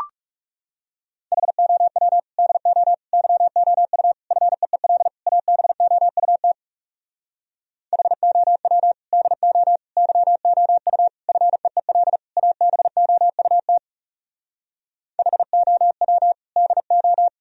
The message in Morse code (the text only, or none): HOW DO YOU FEEL ABOUT HOW DO YOU FEEL ABOUT HOW DO